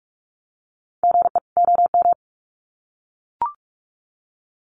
Morse code, heard as a keyed tone, about 44 wpm, 700 Hz. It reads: GE OM
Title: Common QSO Phrases 45wpm